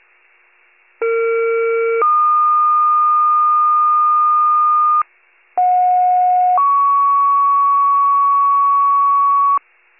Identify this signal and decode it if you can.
Two-tone page, heard as a one-second tone, then a three-second tone, about 460 then 1220 Hz, about 730 then 1120 Hz. It is heard from 1.0 to 9.6 s.